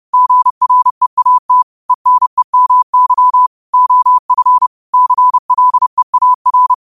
Morse code, heard as a keyed tone, about 30 wpm, 1000 Hz. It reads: GREAT ENEMY OF CLEAR
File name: qotdfiles / 30wpm